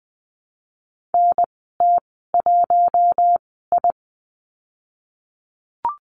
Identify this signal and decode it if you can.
Morse code, heard as a keyed tone, about 20 wpm, spaced as for 14 wpm, 700 Hz. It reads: NT1I
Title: 055 Lesson 10g - Letter r - Call Signs 20wpm